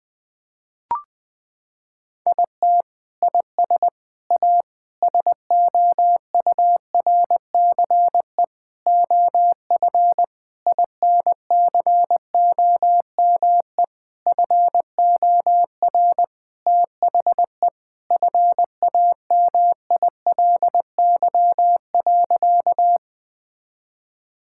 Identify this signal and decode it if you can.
Morse code, heard as a keyed tone, about 21 wpm, 700 Hz. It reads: IT IS A SOURCE OF INCOME FOR THE FAMILY.
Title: Sentences from Top 1000 Words - Review Part 5 - 20wpm